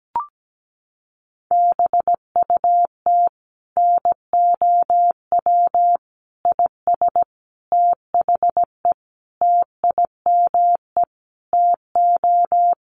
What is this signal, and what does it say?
Morse code, heard as a keyed tone, about 17 wpm, 700 Hz. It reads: BUT NOW IS THE TIME TO
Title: Sentences from Top 1500 Words 17wpm